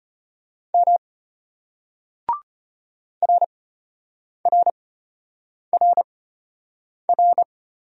Morse code, heard as a keyed tone, about 34 wpm, spaced as for 8 wpm, 690 Hz. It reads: M RRRR